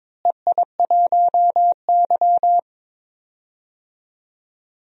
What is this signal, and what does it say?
Morse code, heard as a keyed tone, about 22 wpm, 700 Hz. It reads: EI1Y
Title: DXCC-100 Call Signs - 4 Characters - Repeated 2x - 22wpm